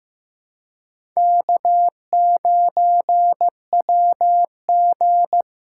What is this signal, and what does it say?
Morse code, heard as a keyed tone, about 15 wpm, 700 Hz. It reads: K9WG
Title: USA Call Signs - 4 Characters - 2x Character Spacing - 15wpm